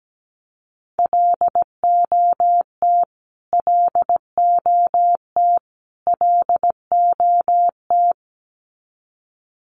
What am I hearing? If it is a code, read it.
Morse code, heard as a keyed tone, about 17 wpm, 700 Hz. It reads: LOT LOT LOT
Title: Top 500 Words Repeated 17wpm